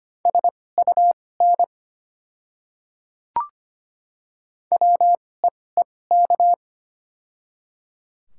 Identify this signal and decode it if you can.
Morse code, heard as a keyed tone, about 25 wpm, spaced as for 18 wpm, 700 Hz. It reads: SUN WEEK
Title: 2x Character Spacing - Top 500 Words - 25wpm